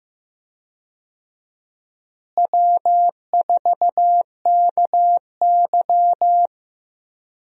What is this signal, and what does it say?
Morse code, heard as a keyed tone, about 15 wpm, 700 Hz. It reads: W4KY
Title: USA Call Signs - 4 Characters - Repeated 2x - 15wpm